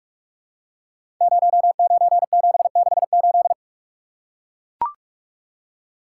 Morse code, heard as a keyed tone, about 44 wpm, 700 Hz. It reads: 09768
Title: Code Groups - 5 Letter:Numbers - 45wpm